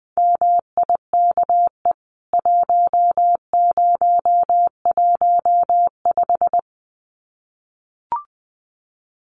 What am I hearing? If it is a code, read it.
Morse code, heard as a keyed tone, about 20 wpm, 700 Hz. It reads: MIKE 1015
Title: CWT Exchange 20wpm